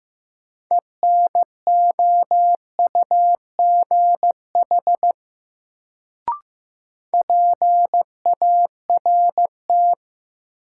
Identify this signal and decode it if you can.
Morse code, heard as a keyed tone, about 15 wpm, 700 Hz. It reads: ENOUGH PART
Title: Sets of 2 Words - New Word Drill - 15wpm